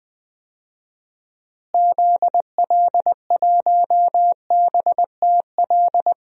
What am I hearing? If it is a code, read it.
Morse code, heard as a keyed tone, about 21 wpm, 700 Hz. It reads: ZL1BTL
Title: DXCC-100 Call Signs - 6 Characters - Repeated 3x - 20wpm